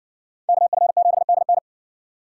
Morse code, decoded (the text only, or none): KF7DN